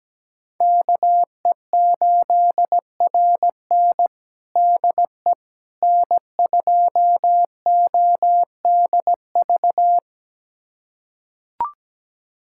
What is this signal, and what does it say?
Morse code, heard as a keyed tone, about 17 wpm, 700 Hz. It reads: KE8RN DE N2ODV
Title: USA Callsign DE Callsign - 5 Characters - 17wpm